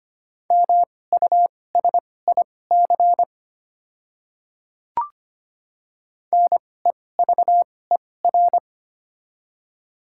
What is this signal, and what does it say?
Morse code, heard as a keyed tone, about 25 wpm, spaced as for 18 wpm, 700 Hz. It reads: MUSIC NEVER
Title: 2x Character Spacing - Top 300 Words - 25wpm